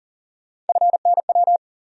Morse code, heard as a keyed tone, about 40 wpm, 700 Hz. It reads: FNW
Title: USA Call Sign Postfixes - 3 Letters 40wpm